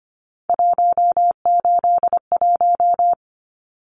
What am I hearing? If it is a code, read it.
Morse code, heard as a keyed tone, about 25 wpm, 700 Hz. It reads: 181
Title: Numbers - 3-Digit 25wpm